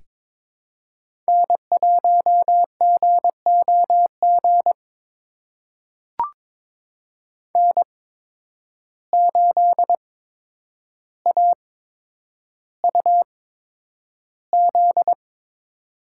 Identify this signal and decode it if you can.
Morse code, heard as a keyed tone, about 22 wpm, spaced as for 10 wpm, 700 Hz. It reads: N1GOG N8AUZ
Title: USA Call Signs - 5 Characters - 8x Character Spacing - 22wpm